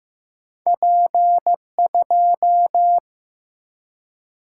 Morse code, heard as a keyed tone, about 15 wpm, 700 Hz. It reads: P2
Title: Single-Letter Single-Number 15wpm